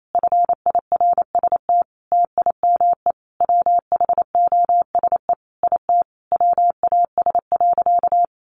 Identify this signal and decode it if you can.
Morse code, heard as a keyed tone, about 28 wpm, 700 Hz. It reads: FIRST TIME WHOSE IT WAS.
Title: Sentences from Top 1200 Words 28wpm